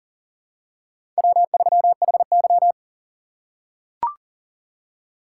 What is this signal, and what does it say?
Morse code, heard as a keyed tone, about 40 wpm, 700 Hz. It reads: W3HY
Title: USA Call Signs - 4 Characters - 4x Character Spacing - 40wpm